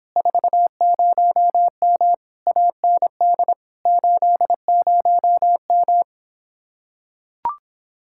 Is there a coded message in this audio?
Yes: Morse code from 0.2 to 6.0 s, heard as a keyed tone, about 26 wpm, 700 Hz. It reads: 40M AND 80M